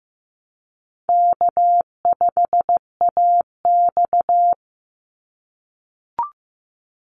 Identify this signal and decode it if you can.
Morse code, heard as a keyed tone, about 15 wpm, 700 Hz. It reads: K5AX
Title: Easy Call Signs - 15wpm